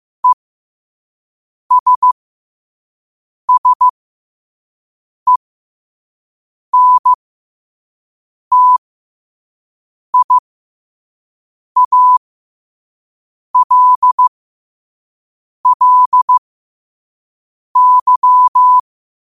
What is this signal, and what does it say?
Morse code, heard as a keyed tone, about 15 wpm, spaced as for 5 wpm, 1000 Hz. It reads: ESSENTIALLY
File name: qotdfiles / 5wpm